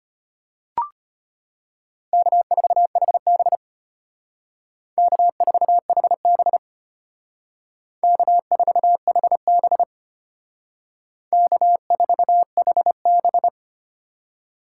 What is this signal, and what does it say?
Morse code, heard as a keyed tone, about 32 wpm, 700 Hz. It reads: K4HB K4HB K4HB K4HB